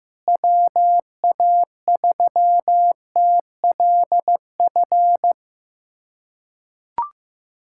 Morse code, heard as a keyed tone, about 15 wpm, 700 Hz. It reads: WA3TLF
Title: USA Callsign DE Callsign 15wpm